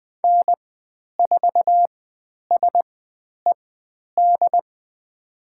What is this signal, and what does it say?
Morse code, heard as a keyed tone, about 20 wpm, spaced as for 10 wpm, 700 Hz. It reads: N4SED
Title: 095 Lesson 16g - Letter c - Call Signs 20wpm